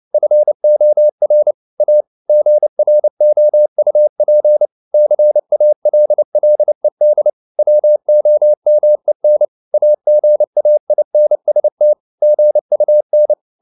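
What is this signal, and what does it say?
Morse code, heard as a keyed tone, about 29 wpm, 600 Hz. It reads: FOR A GROUP CALLED WOMEN AGAINST GUN
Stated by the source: Morse code at 29 wpm, sending Ken Follett's novel Never in English, from CW med Gnister.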